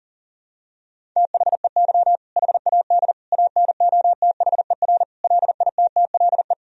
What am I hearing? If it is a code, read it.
Morse code, heard as a keyed tone, about 40 wpm, 700 Hz. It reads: THEY HAD ANOTHER LITTLE
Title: Sentences from Top 1000 Words - Review Part 1 - 40wpm